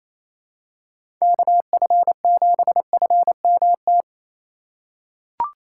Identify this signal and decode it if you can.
Morse code, heard as a keyed tone, about 28 wpm, 700 Hz. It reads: KF7FMT
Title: USA Call Signs - 6 Characters - 2x Character Spacing - 28wpm